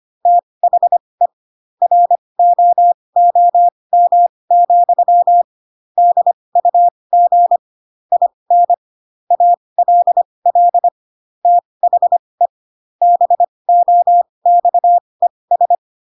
Morse code, heard as a keyed tone, about 25 wpm, spaced as for 20 wpm, 700 Hz. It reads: THE ROOM, DUG IN ALL THE BOXES